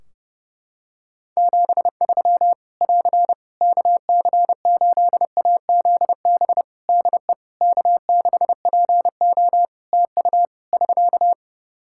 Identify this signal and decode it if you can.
Morse code, heard as a keyed tone, about 30 wpm, 700 Hz. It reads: 73 <AR> KC8AZB DE K6PO TU <SK>
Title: Practice QSO - Exchange 4 - Ending - 30wpm